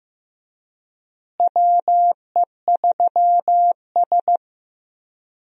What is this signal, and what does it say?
Morse code, heard as a keyed tone, about 15 wpm, 700 Hz. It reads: WE3S
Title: USA Call Signs - 4 Characters - Repeated 2x - 15wpm